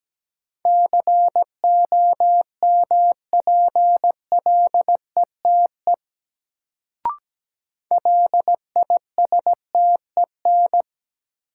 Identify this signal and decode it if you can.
Morse code, heard as a keyed tone, about 17 wpm, 700 Hz. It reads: COMPLETE LISTEN
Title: Sentences from Top 400 Words - New Word Drill - 17wpm